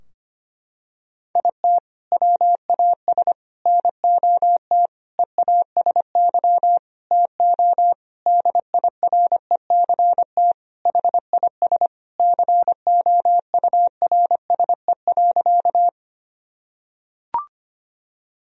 Morse code, heard as a keyed tone, about 25 wpm, 700 Hz. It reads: IT WAS NOT EASY TO DIRECT HIS COURSE.
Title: Sentences from Top 1000 Words - Review Part 2 - 25wpm